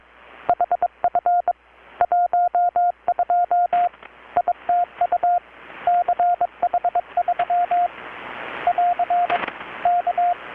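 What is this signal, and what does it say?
Morse code, heard as a keyed tone, about 23 wpm, 720 Hz, with radio noise. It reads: HF 12 ITU CH3 <AR> K
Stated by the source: Morse from KPH's Night of Nights XX maritime radiotelegraphy event, received in Chiapas, Mexico about 3,800 km away on a Kenwood R-600 with a grounded 10 m long-wire antenna.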